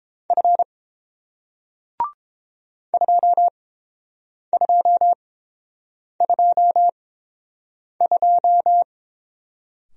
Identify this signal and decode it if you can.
Morse code, heard as a keyed tone, about 30 wpm, spaced as for 7 wpm, 700 Hz. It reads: F 2222